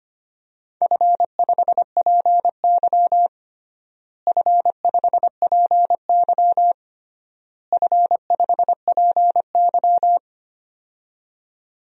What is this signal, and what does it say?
Morse code, heard as a keyed tone, about 25 wpm, 700 Hz. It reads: F5PY F5PY F5PY
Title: DXCC-100 Call Signs - 4 Characters - Repeated 3x - 25wpm